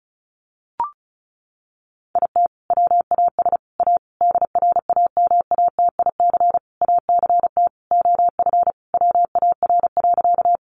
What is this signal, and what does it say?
Morse code, heard as a keyed tone, about 34 wpm, 700 Hz. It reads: IT WAS A DRAMATIC ACT OF WAR.
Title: Sentences from Top 1300 Words 35wpm